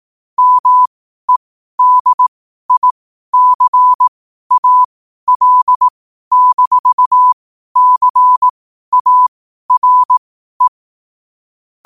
Morse code, heard as a keyed tone, about 18 wpm, spaced as for 12 wpm, 1000 Hz. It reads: MEDICAL-CARE
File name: qotdfiles / 12wpm